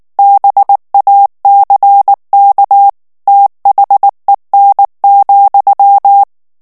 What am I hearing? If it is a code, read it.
Morse code, heard as a keyed tone, about 19 wpm, 800 Hz. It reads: BACK THEN,
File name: ANGLAIS texte clair 3 vitesse 18 mots minute